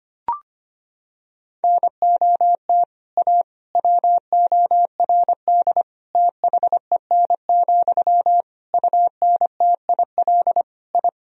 Morse code, heard as a keyed tone, about 25 wpm, 700 Hz. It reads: NOT A WORD THEN, UNTIL I
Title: Sentences from Top 1000 Words - Review Part 1 - 25wpm